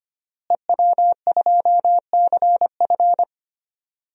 Morse code, heard as a keyed tone, about 25 wpm, 700 Hz. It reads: EW2CF
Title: DXCC-100 Call Signs - 5 Characters - Repeated 3x - 25wpm